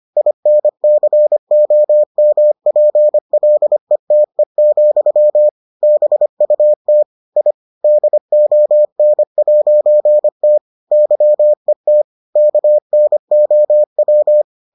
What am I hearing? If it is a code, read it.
Morse code, heard as a keyed tone, about 25 wpm, 600 Hz. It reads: INCOMPLETE, BUT I DON'T YET KNOW